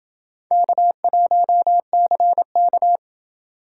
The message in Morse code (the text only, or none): K1CK